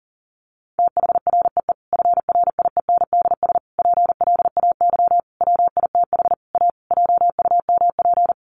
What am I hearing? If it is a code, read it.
Morse code, heard as a keyed tone, about 40 wpm, 700 Hz. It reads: THREE FRIENDS PLAY WITH A JUMP
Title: Easy Phrases 40wpm